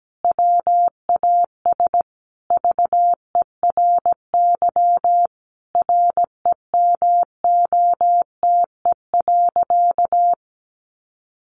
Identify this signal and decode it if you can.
Morse code, heard as a keyed tone, about 17 wpm, 700 Hz. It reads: WAS VERY REMOTE.
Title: Sentences from Top 1300 Words 17wpm